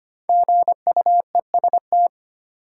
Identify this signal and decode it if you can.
Morse code, heard as a keyed tone, about 25 wpm, 700 Hz. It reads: GUEST
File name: 5 Letter Words - First 3 Letters Given - 25wpm